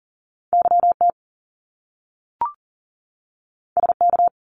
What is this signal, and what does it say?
Morse code, heard as a keyed tone, about 38 wpm, 700 Hz. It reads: YT SK
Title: Canadian Provinces and Territories 40wpm